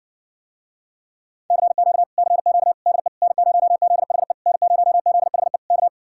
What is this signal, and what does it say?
Morse code, heard as a keyed tone, about 60 wpm, 700 Hz. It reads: CQ CQ DE N0ZHE N0ZHE K